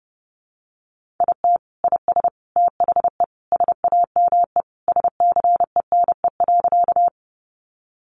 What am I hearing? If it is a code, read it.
Morse code, heard as a keyed tone, about 30 wpm, 700 Hz. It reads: IT IS THE SAME SCENE.
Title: Sentences from Top 1100 Words 30wpm